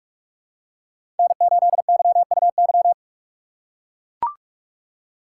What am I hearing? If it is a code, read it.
Morse code, heard as a keyed tone, about 43 wpm, 700 Hz. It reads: N8YUY
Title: USA Call Signs - 5 Characters - 6x Character Spacing - 45wpm